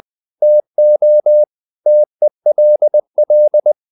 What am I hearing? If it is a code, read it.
Morse code, heard as a keyed tone, about 20 wpm, 600 Hz. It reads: TO TELL
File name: the old man and the sea part 2 20wpm